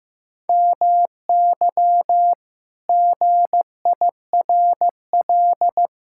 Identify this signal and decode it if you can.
Morse code, heard as a keyed tone, about 15 wpm, 700 Hz. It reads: MY GIRL
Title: Pangram Challenge v2 - 15wpm